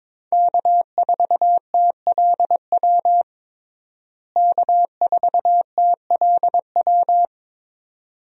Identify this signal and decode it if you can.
Morse code, heard as a keyed tone, about 22 wpm, 700 Hz. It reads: K4TLW K4TLW
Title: USA Call Signs - 5 Characters - Repeated 2x - 22wpm